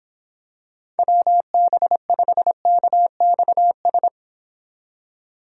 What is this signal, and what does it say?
Morse code, heard as a keyed tone, about 26 wpm, 700 Hz. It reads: WB5KXS